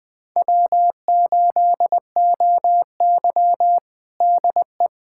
Morse code, heard as a keyed tone, about 20 wpm, 700 Hz. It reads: W8OY DE
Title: USA Callsign DE Callsign - 4 Characters - 20wpm